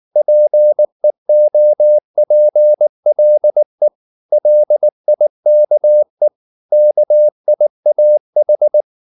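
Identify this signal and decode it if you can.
Morse code, heard as a keyed tone, about 19 wpm, 600 Hz. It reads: PEOPLE LIKE KIAH
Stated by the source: Morse code at 19 wpm from CW med Gnister, reading Ken Follett's novel Never in English.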